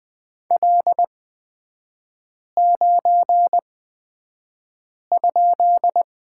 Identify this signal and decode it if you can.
Morse code, heard as a keyed tone, about 20 wpm, spaced as for 5 wpm, 700 Hz. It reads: L9?
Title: 255 Lesson 41e - Period - Three Characters 20wpm